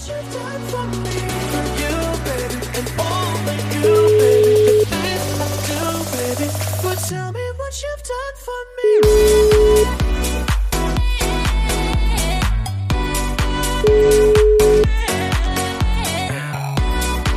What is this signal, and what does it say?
Ringback tone (Europe, 425 Hz), with noise.